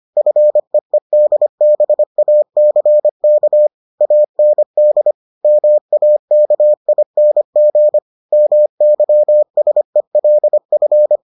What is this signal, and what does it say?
Morse code, heard as a keyed tone, about 25 wpm, 600 Hz. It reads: FEEDBACK AND MAKING MYSELF